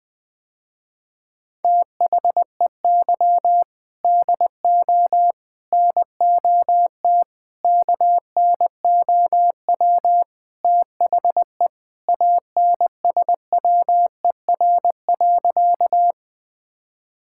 Morse code, heard as a keyed tone, about 20 wpm, 700 Hz. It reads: THEY DO NOT KNOW THE ANSWER.
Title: Sentences with 4-Word Phrases - Spoken Beginning - 20wpm